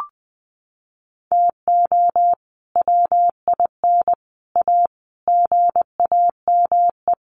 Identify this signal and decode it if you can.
Morse code, heard as a keyed tone, about 20 wpm, 700 Hz. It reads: TO WIN A GAME
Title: Sentences from Top 1500 Words 20wpm